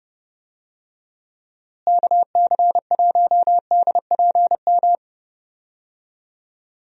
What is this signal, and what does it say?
Morse code, heard as a keyed tone, about 30 wpm, 700 Hz. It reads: KC1DPM
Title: USA Call Signs - 6 Characters - Repeated 3x - 30wpm